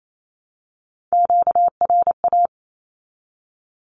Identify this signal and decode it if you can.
Morse code, heard as a keyed tone, about 28 wpm, 700 Hz. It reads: QRA
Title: Q-Codes 28wpm